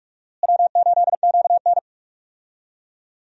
Morse code, heard as a keyed tone, about 46 wpm, 700 Hz. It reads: W8QN
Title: USA Call Signs - 4 Characters - Repeated 3x - 45wpm